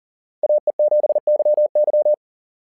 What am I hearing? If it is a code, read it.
Morse code, heard as a keyed tone, about 38 wpm, 590 Hz. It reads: AE7YY
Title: SOTA QSO Practice 40wpm